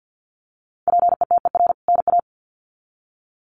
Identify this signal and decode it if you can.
Morse code, heard as a keyed tone, about 50 wpm, 700 Hz. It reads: PETER NA
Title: CWT Exchange 50wpm